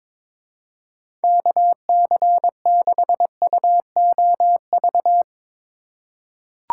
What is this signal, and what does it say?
Morse code, heard as a keyed tone, about 22 wpm, 700 Hz. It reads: KC6UOV
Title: USA Call Signs - 6 Characters - 4x Character Spacing - 22wpm